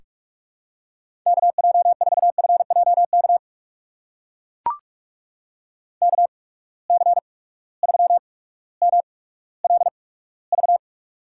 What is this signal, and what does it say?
Morse code, heard as a keyed tone, about 44 wpm, spaced as for 19 wpm, 700 Hz. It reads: KJ4FJK KC3MLV
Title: USA Call Signs - 6 Characters - 8x Character Spacing - 45wpm